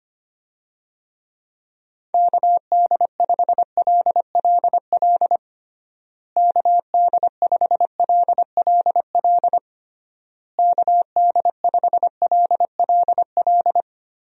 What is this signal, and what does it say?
Morse code, heard as a keyed tone, about 25 wpm, 700 Hz. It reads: KD5LLL KD5LLL KD5LLL
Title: USA Call Signs - 6 Characters - Repeated 3x - 25wpm